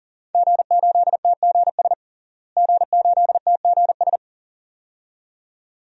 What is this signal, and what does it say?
Morse code, heard as a keyed tone, about 37 wpm, 700 Hz. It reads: G8TGS G8TGS
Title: DXCC-100 Call Signs - 5 Characters - Repeated 2x - 40wpm